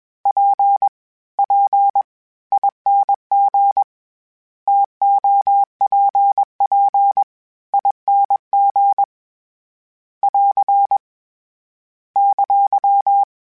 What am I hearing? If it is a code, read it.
Morse code, heard as a keyed tone, about 21 wpm, spaced as for 17 wpm, 800 Hz. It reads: PPING TOPPING <AR> !
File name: Variable Speed / Variable Speed-mp3s / TO sp @20